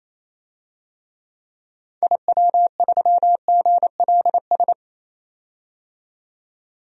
Morse code, heard as a keyed tone, about 27 wpm, 700 Hz. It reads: IW3GLS
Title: DXCC-100 Call Signs - 6 Characters - 28wpm